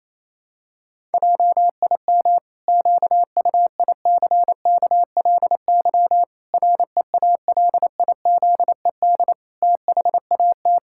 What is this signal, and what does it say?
Morse code, heard as a keyed tone, about 28 wpm, 700 Hz. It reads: JIM QUICKLY REALIZED THAT
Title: Pangram Challenge v2 - 28wpm